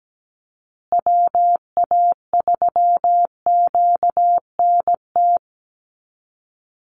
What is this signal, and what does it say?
Morse code, heard as a keyed tone, about 17 wpm, 700 Hz. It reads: WA3QNT
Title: Call Signs Repeated 2x - 17wpm